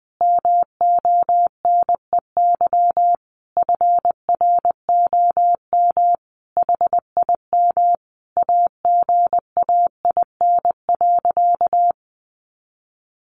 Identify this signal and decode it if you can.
Morse code, heard as a keyed tone, about 20 wpm, 700 Hz. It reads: MONEY FROM HIM AGAIN.
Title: Sentences from Top 1300 Words 20wpm